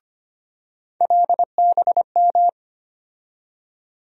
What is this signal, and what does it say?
Morse code, heard as a keyed tone, about 25 wpm, 700 Hz. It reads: LBM
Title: USA Call Sign Postfixes - 3 Letters 25wpm